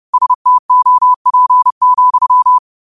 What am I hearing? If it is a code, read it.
Morse code, heard as a keyed tone, about 30 wpm, 1000 Hz. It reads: STOP,